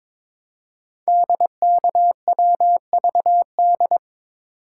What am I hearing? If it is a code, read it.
Morse code, heard as a keyed tone, about 22 wpm, 700 Hz. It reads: DKWVD
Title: Code Groups - 5 Letters - 22wpm